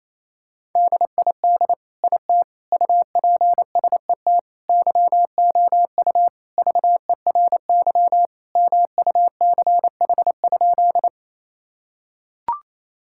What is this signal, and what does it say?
Morse code, heard as a keyed tone, about 28 wpm, 700 Hz. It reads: DID IT UPSET YOU VERY MUCH?
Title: Sentences from Top 1000 Words - Review Part 5 - 28wpm